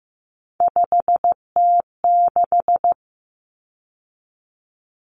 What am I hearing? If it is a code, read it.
Morse code, heard as a keyed tone, about 15 wpm, 700 Hz. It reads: 5T6
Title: Common Cut Numbers - 3-Digit 15wpm